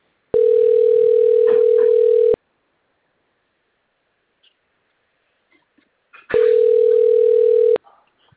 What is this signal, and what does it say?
Ringback tone (North America), 440 + 480 Hz.